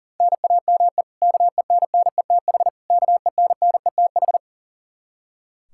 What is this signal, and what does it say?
Morse code, heard as a keyed tone, about 40 wpm, 700 Hz. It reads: NAME KENNETH KENNETH
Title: Practice QSO - Exchange 1 - Beginning - 40wpm